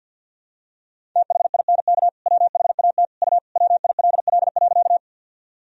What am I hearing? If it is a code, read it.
Morse code, heard as a keyed tone, about 50 wpm, 700 Hz. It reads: THINK WHAT U WILL.
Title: Sentences from Top 200 Words and Abbreviations 50wpm